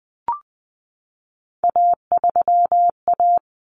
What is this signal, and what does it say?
Morse code, heard as a keyed tone, about 20 wpm, 700 Hz. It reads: A3A
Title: Common Cut Numbers - 3-Digit 20wpm